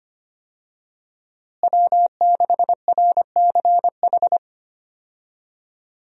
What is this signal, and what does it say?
Morse code, heard as a keyed tone, about 25 wpm, 700 Hz. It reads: W6RCH
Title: USA Call Signs - 5 Characters - Repeated 2x - 25wpm